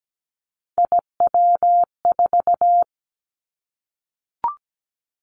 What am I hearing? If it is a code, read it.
Morse code, heard as a keyed tone, about 17 wpm, 700 Hz. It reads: IW4
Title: Callsign Prefixes for Top 25 DXCC - 17wpm